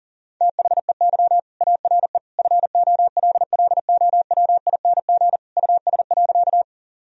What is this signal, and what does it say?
Morse code, heard as a keyed tone, about 40 wpm, 700 Hz. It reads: THEY ARE FOLLOWING US.
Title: Sentences from Top 1000 Words - Review Part 4 - 40wpm